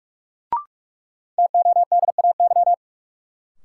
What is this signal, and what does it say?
Morse code, heard as a keyed tone, about 42 wpm, 700 Hz. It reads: TODAY
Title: Sets of 5 Words Encore - New Word Drill - 45wpm